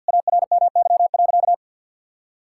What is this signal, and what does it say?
Morse code, heard as a keyed tone, about 52 wpm, 700 Hz. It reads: ARMY.
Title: Sentences from Top 1000 Words - Review Part 6 - 50wpm